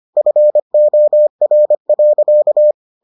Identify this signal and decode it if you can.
Morse code, heard as a keyed tone, about 25 wpm, 600 Hz. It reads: FOR.